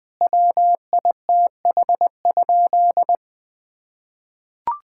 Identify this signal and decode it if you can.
Morse code, heard as a keyed tone, about 20 wpm, 700 Hz. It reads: WITH?
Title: Sentences from Top 1000 Words - Review Part 5 - 20wpm